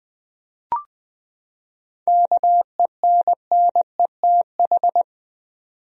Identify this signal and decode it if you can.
Morse code, heard as a keyed tone, about 20 wpm, 700 Hz. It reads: KENNETH
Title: USA Common Names - 20wpm